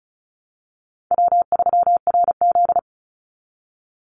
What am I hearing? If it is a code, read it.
Morse code, heard as a keyed tone, about 34 wpm, 700 Hz. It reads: W3RZ
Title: Easy Call Signs Repeated 2x - 35wpm